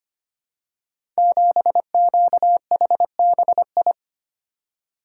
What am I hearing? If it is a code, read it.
Morse code, heard as a keyed tone, about 25 wpm, 700 Hz. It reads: 7QHBI
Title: Code Groups - 5 Letter:Numbers - 25wpm